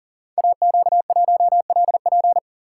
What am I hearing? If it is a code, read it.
Morse code, heard as a keyed tone, about 38 wpm, 700 Hz. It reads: AQ1LP
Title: Code Groups - 5 Letter:Numbers - 40wpm